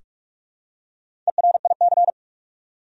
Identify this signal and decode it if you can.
Morse code, heard as a keyed tone, about 46 wpm, 700 Hz. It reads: ERIC
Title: USA Common Names - Max 4 Letters - 45wpm